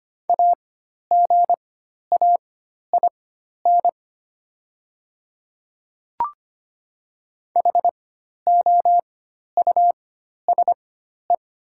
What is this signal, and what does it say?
Morse code, heard as a keyed tone, about 25 wpm, spaced as for 12 wpm, 700 Hz. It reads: AGAIN HOUSE
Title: 4x Character Spacing - Top 200 Words - 25wpm